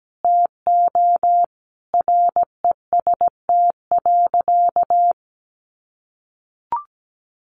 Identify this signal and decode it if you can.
Morse code, heard as a keyed tone, about 17 wpm, 700 Hz. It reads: TO REST.
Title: Sentences from Top 600 Words 17wpm